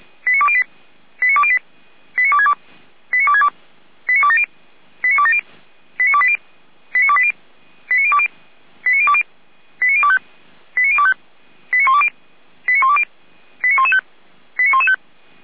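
Five-tone selective call (ZVEI), heard as quick runs of five tones, tones of about 70 ms. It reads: 89208 89208 89262 89262 89270 89270 89280 89280 89020 89020 89025 89025 89120 89120 89115 89115
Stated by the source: ZVEI five-tone selective call received on an SDR.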